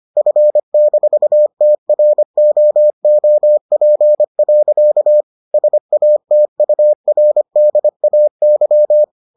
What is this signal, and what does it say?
Morse code, heard as a keyed tone, about 25 wpm, 600 Hz. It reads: F-TROOP. SATURDAY